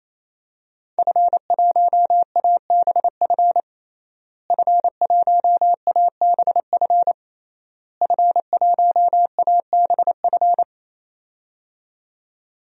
Morse code, heard as a keyed tone, about 28 wpm, 700 Hz. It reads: F1ABF F1ABF F1ABF
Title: DXCC-100 Call Signs - 5 Characters - Repeated 3x - 28wpm